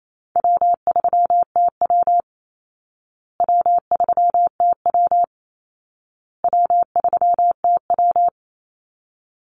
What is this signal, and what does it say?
Morse code, heard as a keyed tone, about 28 wpm, 700 Hz. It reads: W3TW W3TW W3TW
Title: Easy Call Signs Repeated 3x - 28wpm